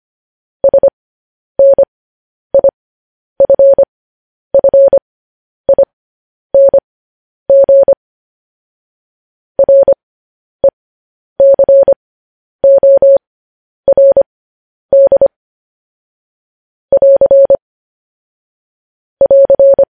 Morse code, heard as a keyed tone, about 25 wpm, spaced as for 10 wpm, 550 Hz. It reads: SNIFFING RECORD <AR> <AR>
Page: news headlines in morse code at 10 wpm